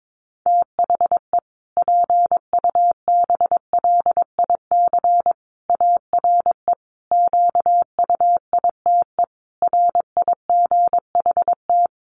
Morse code, heard as a keyed tone, about 22 wpm, 700 Hz. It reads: THE PUBLIC ARE QUITE RIGHT
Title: Sentences from Top 1300 Words 22wpm